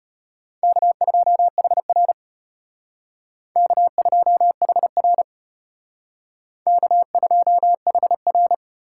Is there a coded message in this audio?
Yes: Morse code from 0.6 to 8.5 s, heard as a keyed tone, about 32 wpm, 700 Hz. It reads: K2HR K2HR K2HR